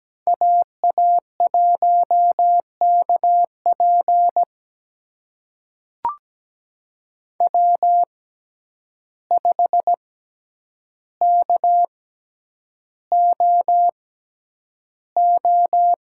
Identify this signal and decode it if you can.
Morse code, heard as a keyed tone, about 17 wpm, spaced as for 9 wpm, 700 Hz. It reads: AA1KP W5KOO
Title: USA Call Signs - 6x Character Spacing - 17wpm